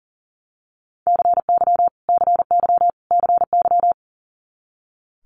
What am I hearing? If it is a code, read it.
Morse code, heard as a keyed tone, about 40 wpm, 700 Hz. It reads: CY CY CY
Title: Multi-letter Phonemes Repeated 3x 40wpm